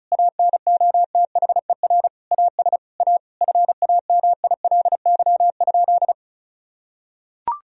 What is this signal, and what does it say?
Morse code, heard as a keyed tone, about 34 wpm, 700 Hz. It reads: ANOTHER AS A FAMILY?
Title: Sentences from Top 1000 Words - Review Part 1 - 35wpm